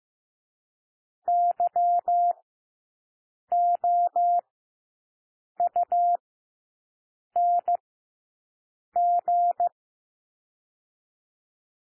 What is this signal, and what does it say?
Morse code, heard as a keyed tone, about 15 wpm, spaced as for 6 wpm, 700 Hz. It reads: YOUNG
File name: cwmp3 / W1 SOUND WORDS 15 wpm